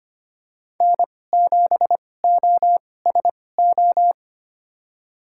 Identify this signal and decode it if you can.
Morse code, heard as a keyed tone, about 25 wpm, spaced as for 18 wpm, 700 Hz. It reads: N7OSO
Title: USA Call Signs - 5 Characters - 2x Character Spacing - 25wpm